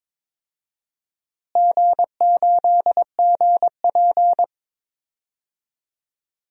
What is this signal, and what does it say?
Morse code, heard as a keyed tone, about 22 wpm, 700 Hz. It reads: G8GP
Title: DXCC-100 Call Signs - 4 Characters - Repeated 2x - 22wpm